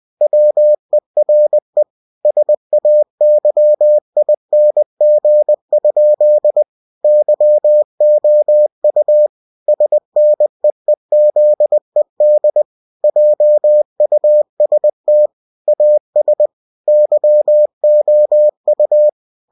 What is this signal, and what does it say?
Morse code, heard as a keyed tone, about 20 wpm, 600 Hz. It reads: WERE SAYING? YOU SNEEZED JUST AS YOU
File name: cw-books / winnie-the-pooh / winnie-20wpm-page-0030